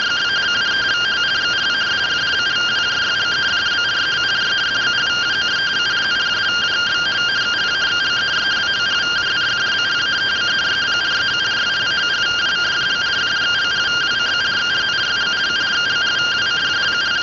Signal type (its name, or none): two-tone teleprinter signal